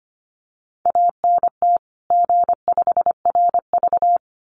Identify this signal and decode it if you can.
Morse code, heard as a keyed tone, about 25 wpm, 700 Hz. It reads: ANT G5RV
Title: Common QSO Phrases 25wpm